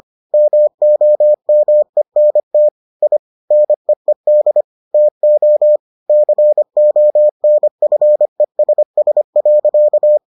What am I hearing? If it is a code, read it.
Morse code, heard as a keyed tone, about 25 wpm, 600 Hz. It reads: MOMENT I NEED TO CONFESS.